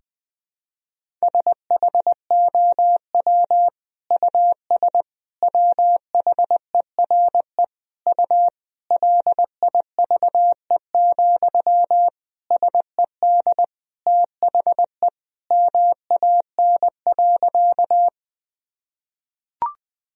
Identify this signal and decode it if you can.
Morse code, heard as a keyed tone, about 20 wpm, 700 Hz. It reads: SHOW US WHERE U LIVE, SED THE MAN.
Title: Sentences from Top 200 Words and Abbreviations 20wpm